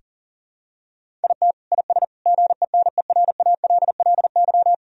Morse code, heard as a keyed tone, about 40 wpm, 700 Hz. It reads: IT IS GENERALLY
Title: Sentences from Top 1000 Words - Review Part 4 - 40wpm